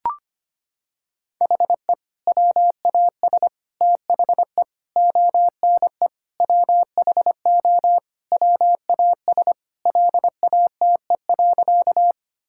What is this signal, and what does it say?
Morse code, heard as a keyed tone, about 25 wpm, 700 Hz. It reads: HE WAS THE ONE WHO WAS LATE.
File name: practice / 25wpm / Sentences with 2-Word Phrases - 25wpm